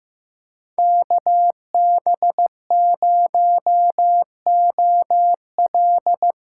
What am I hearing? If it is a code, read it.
Morse code, heard as a keyed tone, about 15 wpm, 700 Hz. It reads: KB0OL
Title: USA Call Signs - 6 Characters - 8x Character Spacing - 15wpm